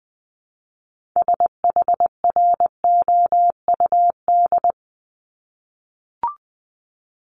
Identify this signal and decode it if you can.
Morse code, heard as a keyed tone, about 20 wpm, 700 Hz. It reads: SHROUD
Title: Multi-letter Phonemes Repeated 3x 20wpm